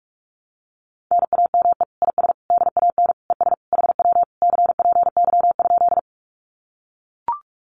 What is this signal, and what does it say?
Morse code, heard as a keyed tone, about 44 wpm, 700 Hz. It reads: NAME IS DAN ES HW CPY?
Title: Common QSO Phrases Encore 45wpm